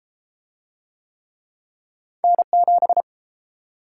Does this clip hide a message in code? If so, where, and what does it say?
Yes: Morse code from 2.2 to 3.0 s, heard as a keyed tone, about 30 wpm, 700 Hz. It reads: N7